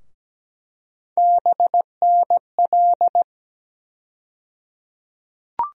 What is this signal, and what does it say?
Morse code, heard as a keyed tone, about 17 wpm, 700 Hz. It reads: BNL
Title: USA Call Sign Postfixes - 3 Letters 17wpm